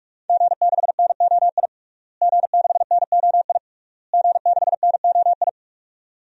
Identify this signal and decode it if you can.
Morse code, heard as a keyed tone, about 48 wpm, 700 Hz. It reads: G6NOI G6NOI G6NOI
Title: DXCC-100 Call Signs - 5 Characters - Repeated 3x - 45wpm